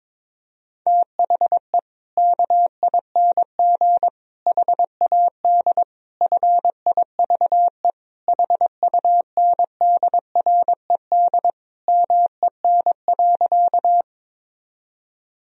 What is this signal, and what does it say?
Morse code, heard as a keyed tone, about 22 wpm, 700 Hz. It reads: THE KING HAD FIVE HUNDRED MEN.
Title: Sentences from Top 1000 Words - Review Part 2 - 22wpm